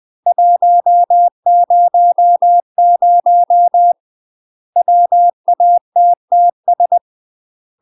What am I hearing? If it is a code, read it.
Morse code, heard as a keyed tone, about 20 wpm, 700 Hz. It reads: 100 WATTS
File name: UK MASTER QSOs TRAINING / 20WPM / UK MASTER QSOs 0034